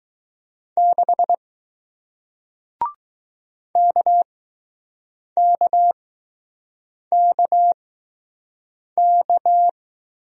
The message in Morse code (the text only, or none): 6 KKKK